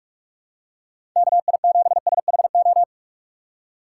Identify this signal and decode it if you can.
Morse code, heard as a keyed tone, about 46 wpm, 700 Hz. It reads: KI7SHO
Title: USA Call Signs - 6 Characters - 4x Character Spacing - 45wpm